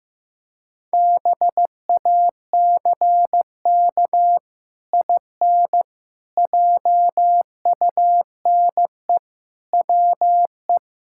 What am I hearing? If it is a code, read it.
Morse code, heard as a keyed tone, about 16 wpm, 700 Hz. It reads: BACK IN JUNE WE
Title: Pangram Challenge v2 - 15wpm